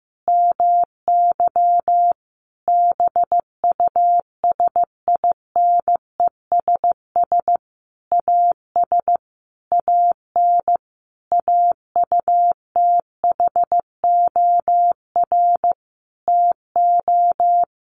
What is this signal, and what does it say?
Morse code, heard as a keyed tone, about 15 wpm, 700 Hz. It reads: MY BUSINESS AS AN AUTHOR TO
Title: Sentences from Top 1200 Words 15wpm